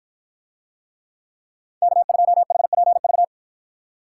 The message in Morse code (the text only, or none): K2HPV